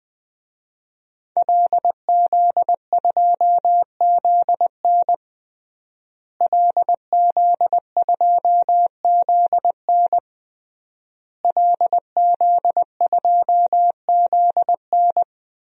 Morse code, heard as a keyed tone, about 21 wpm, 700 Hz. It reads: LZ2ZN LZ2ZN LZ2ZN
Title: DXCC-100 Call Signs - 5 Characters - Repeated 3x - 20wpm